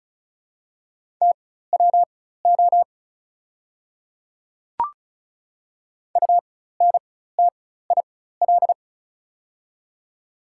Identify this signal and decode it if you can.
Morse code, heard as a keyed tone, about 36 wpm, spaced as for 16 wpm, 700 Hz. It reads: TWO UNTIL
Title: 4x Character Spacing - Top 300 Words - 35wpm